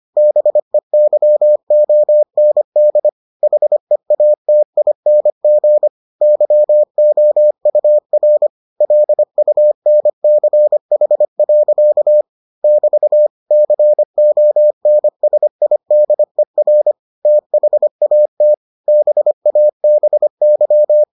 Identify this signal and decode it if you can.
Morse code, heard as a keyed tone, about 25 wpm, 600 Hz. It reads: BEYOND HEATING YOUR LUNCH. <BT> CONSIDER THAT BABY